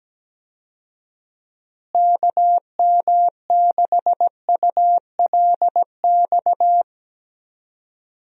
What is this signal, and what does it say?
Morse code, heard as a keyed tone, about 17 wpm, 700 Hz. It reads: KM6ULX
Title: USA Call Signs - 6 Characters - Repeated 2x - 17wpm